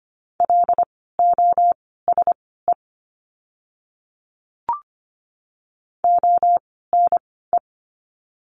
Morse code, heard as a keyed tone, about 25 wpm, spaced as for 16 wpm, 700 Hz. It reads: LOSE ONE
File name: course / 25:10wpm / 094 Lesson 16f - Letter c - Words 25wpm